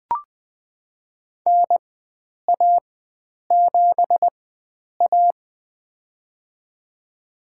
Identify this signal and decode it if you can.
Morse code, heard as a keyed tone, about 20 wpm, spaced as for 9 wpm, 700 Hz. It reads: NA7A
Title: USA Call Signs - 4 Characters - 4x Character Spacing - 20wpm